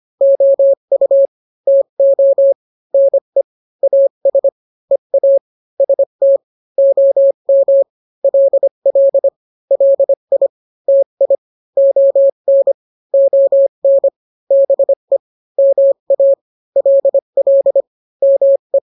Morse code, heard as a keyed tone, about 25 wpm, 550 Hz. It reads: OU TO NE AS EA ST OM LL LI TI ON ON BE MA LL ME